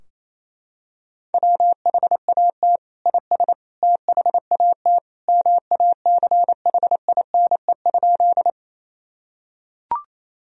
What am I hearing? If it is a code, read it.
Morse code, heard as a keyed tone, about 27 wpm, 700 Hz. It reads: WHAT IS THAT MACHINE?
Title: Sentences from Top 1000 Words - Review Part 2 - 28wpm